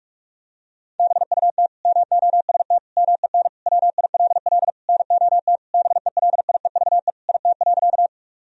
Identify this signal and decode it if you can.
Morse code, heard as a keyed tone, about 41 wpm, 700 Hz. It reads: BUT MOST MEN WILL NOT BELIEVE IT.
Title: Sentences from Top 1000 Words - Review Part 1 - 45wpm